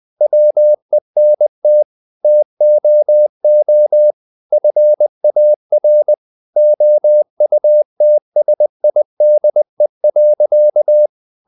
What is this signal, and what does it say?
Morse code, heard as a keyed tone, about 20 wpm, 600 Hz. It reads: WENT TOO FAR OUTSIDE.
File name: the old man and the sea part 2 20wpm